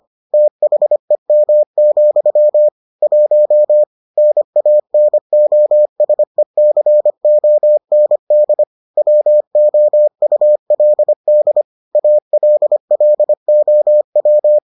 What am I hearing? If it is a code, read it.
Morse code, heard as a keyed tone, about 25 wpm, 600 Hz. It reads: THEM, 1 NANOSECOND WOULD ALLOW